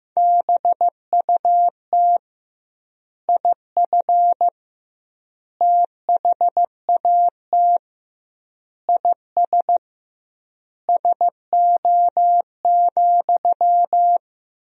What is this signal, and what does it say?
Morse code, heard as a keyed tone, about 15 wpm, 700 Hz. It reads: BUT IF THAT IS SO,
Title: 2x Word-Spacing - Sentences from Top 200 Words - 15wpm